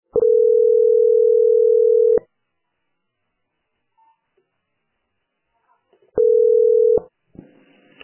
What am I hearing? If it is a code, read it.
Ringback tone (North America), 440 + 480 Hz.